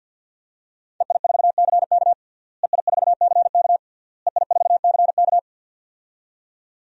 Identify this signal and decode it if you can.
Morse code, heard as a keyed tone, about 47 wpm, 700 Hz. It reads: EI4CK EI4CK EI4CK
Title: DXCC-100 Call Signs - 5 Characters - Repeated 3x - 50wpm